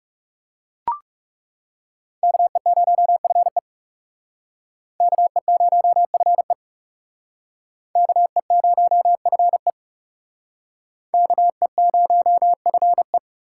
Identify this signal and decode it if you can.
Morse code, heard as a keyed tone, about 37 wpm, 700 Hz. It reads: KE0FE KE0FE KE0FE KE0FE
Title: Speed Racing - USA 2x2 Call Signs 30wpm